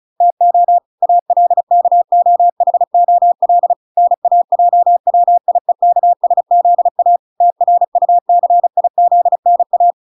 Morse code, heard as a keyed tone, about 36 wpm, 700 Hz. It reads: TO ALKOHOL NAJWIEKSZA TRUCIZNA